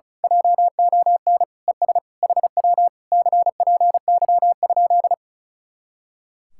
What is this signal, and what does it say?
Morse code, heard as a keyed tone, about 33 wpm, 700 Hz. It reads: JON ES HW CPY?
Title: Speed-Racing Common QSO Phrases Encore - 35wpm